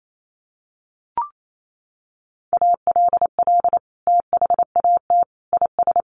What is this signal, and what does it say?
Morse code, heard as a keyed tone, about 28 wpm, 700 Hz. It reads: ALL THAT IS
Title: Sentences from Top 1100 Words 28wpm